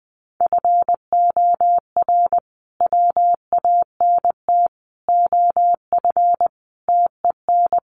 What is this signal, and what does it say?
Morse code, heard as a keyed tone, about 20 wpm, 700 Hz. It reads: FOR WANT OF TEN
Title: A Sound Of Thunder 20wpm